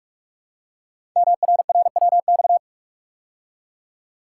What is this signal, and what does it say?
Morse code, heard as a keyed tone, about 45 wpm, 700 Hz. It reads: MRRWX
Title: Code Groups - 5 Letters - 45wpm